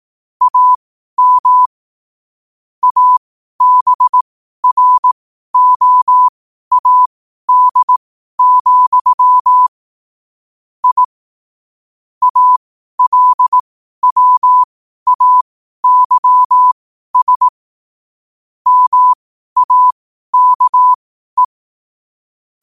Morse code, heard as a keyed tone, about 18 wpm, spaced as for 12 wpm, 1000 Hz. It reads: AM ABROAD, I ALWAYS MAKE
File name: qotdfiles / 12wpm